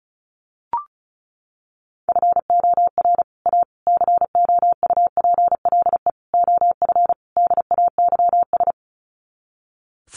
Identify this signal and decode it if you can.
Morse code, heard as a keyed tone, about 34 wpm, 700 Hz. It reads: FOR A COUPLE OF DAYS
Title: Sets of 5 Words Encore2 35wpm